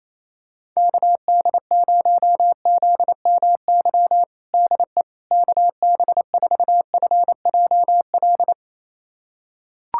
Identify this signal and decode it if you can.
Morse code, heard as a keyed tone, about 27 wpm, 700 Hz. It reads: KD0ZMY DE KB4FJL
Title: USA Callsign DE Callsign - 6 Characters - 28wpm